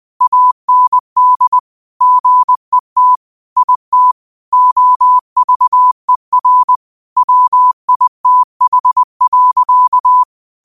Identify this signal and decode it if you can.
Morse code, heard as a keyed tone, about 20 wpm, 1000 Hz. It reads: AND GET IT OVER WITH.
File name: 20wpm